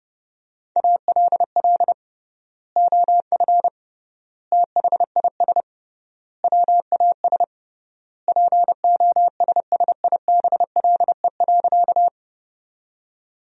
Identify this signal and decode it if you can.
Morse code, heard as a keyed tone, about 32 wpm, 700 Hz. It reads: ALL OF THIS WAS POSSIBLE.
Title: 3x Word-Spacing - Sentences from Top 500 Words - 30wpm